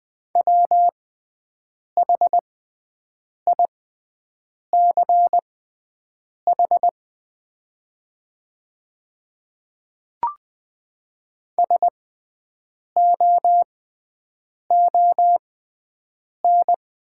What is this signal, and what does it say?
Morse code, heard as a keyed tone, about 20 wpm, spaced as for 7 wpm, 700 Hz. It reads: WHICH SOON
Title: 6x Character Spacing - Top 500 Words - 20wpm